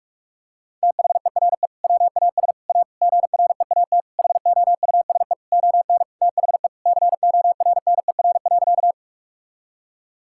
Morse code, heard as a keyed tone, about 41 wpm, 700 Hz. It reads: THERE WAS A GREAT HOUSE ON THE CORNER.
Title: Sentences from Top 1600 Words 45wpm